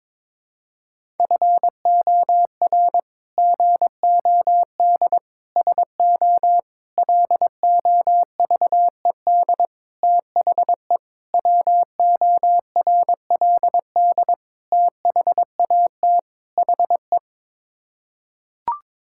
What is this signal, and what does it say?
Morse code, heard as a keyed tone, about 21 wpm, 700 Hz. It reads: FOR GOD SO LOVED THE WORLD THAT HE
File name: practice / 22wpm / Sets of 8 Words - 22wpm